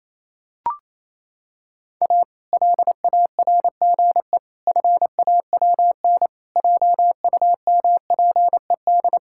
Morse code, heard as a keyed tone, about 28 wpm, 700 Hz. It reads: A LARGE FAWN JUMPED
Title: Pangram Challenge v2 - 28wpm